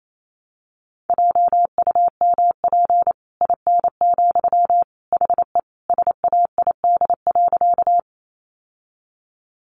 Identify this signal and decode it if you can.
Morse code, heard as a keyed tone, about 28 wpm, 700 Hz. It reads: JUMP IN, HE SAID.
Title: Sentences from Top 1300 Words 28wpm